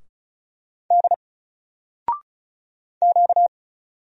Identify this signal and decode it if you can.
Morse code, heard as a keyed tone, about 35 wpm, 700 Hz. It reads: D Q
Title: Single Letter-Number 35wpm